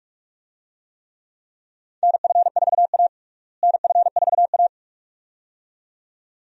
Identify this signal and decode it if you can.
Morse code, heard as a keyed tone, about 46 wpm, 700 Hz. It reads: NF4A NF4A
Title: USA Call Signs - 4 Characters - Repeated 2x - 45wpm